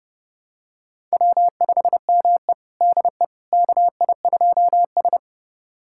Morse code, heard as a keyed tone, about 30 wpm, 700 Hz. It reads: W5ME DE KI2S
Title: USA Callsign DE Callsign - 4 Characters - 30wpm